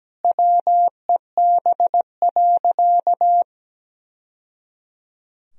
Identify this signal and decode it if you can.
Morse code, heard as a keyed tone, about 17 wpm, 700 Hz. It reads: WEB.
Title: Sentences from Top 1000 Words - Review Part 4 - 17wpm